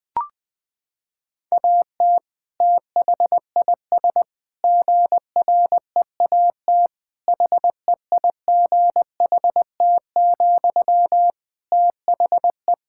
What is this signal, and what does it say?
Morse code, heard as a keyed tone, about 21 wpm, 700 Hz. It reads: AT THIS GREAT HEIGHT, THE
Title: Sentences from Top 1600 Words 20wpm